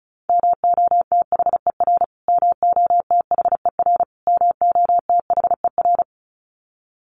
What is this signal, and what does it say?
Morse code, heard as a keyed tone, about 34 wpm, 700 Hz. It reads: MOTHER MOTHER MOTHER
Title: Top 500 Words Repeated 35wpm